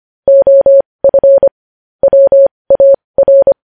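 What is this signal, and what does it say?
Morse code, heard as a keyed tone, about 25 wpm, spaced as for 20 wpm, 550 Hz. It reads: OF WAR
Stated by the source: Morse code news headlines sent at 20.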